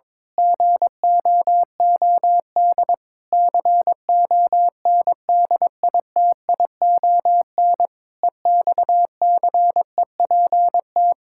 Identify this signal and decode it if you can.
Morse code, heard as a keyed tone, about 22 wpm, 700 Hz. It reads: GOOD CONDITION EXCEPT
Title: Pangram Challenge v2 - 22wpm